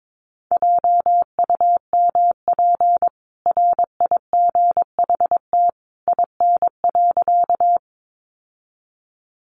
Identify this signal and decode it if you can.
Morse code, heard as a keyed tone, about 22 wpm, 700 Hz. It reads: JUMP RIGHT IN.
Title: Sentences from Top 1300 Words 22wpm